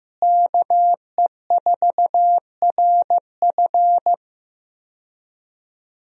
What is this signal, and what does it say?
Morse code, heard as a keyed tone, about 15 wpm, 700 Hz. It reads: KE4RF
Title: USA Call Signs - 5 Characters - Repeated 3x - 15wpm